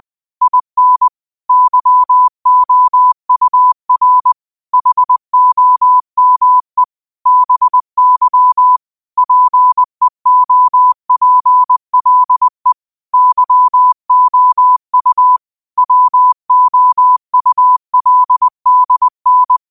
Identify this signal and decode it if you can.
Morse code, heard as a keyed tone, about 20 wpm, 1000 Hz. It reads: IN YOUR HOME BY PEOPLE YOU WOULDN